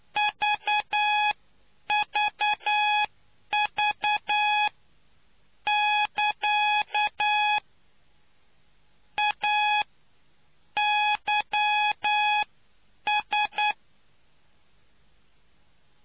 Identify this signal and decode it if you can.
Morse code, heard as a keyed tone, about 9 wpm, spaced as for 8 wpm, 840 Hz. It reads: VVV <KA> A YS